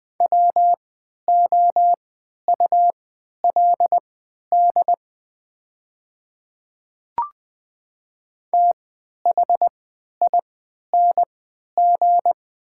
Morse code, heard as a keyed tone, about 20 wpm, spaced as for 11 wpm, 700 Hz. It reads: WOULD THING
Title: 3x Character Spacing - Top 100 Words - 20wpm